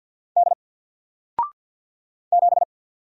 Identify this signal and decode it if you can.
Morse code, heard as a keyed tone, about 50 wpm, 700 Hz. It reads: D 7
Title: Single Character (Letter:Number:Punctuation:Prosign) 50wpm